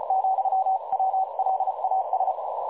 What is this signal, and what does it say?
Morse code, heard as a keyed tone, about 34 wpm, 760 Hz, with noise. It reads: , WLAS
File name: dlazaawansowanych / nagraniazpasma / websdr recording 3565.0kHz